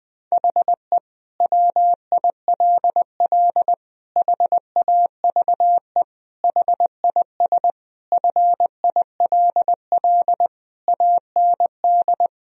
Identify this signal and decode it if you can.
Morse code, heard as a keyed tone, about 20 wpm, 700 Hz. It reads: HE WILL HAVE HIS FILL AND
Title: Sentences from Top 1000 Words - Review Part 3 - 20wpm